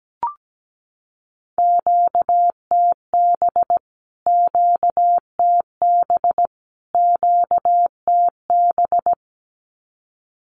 Morse code, heard as a keyed tone, about 17 wpm, 700 Hz. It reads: QTB QTB QTB
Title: Q-Codes Repeated 3x 17wpm